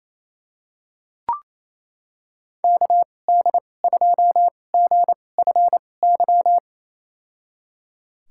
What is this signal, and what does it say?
Morse code, heard as a keyed tone, about 28 wpm, spaced as for 20 wpm, 700 Hz. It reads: KD2GFY
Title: USA Call Signs - 6 Characters - 2x Character Spacing - 28wpm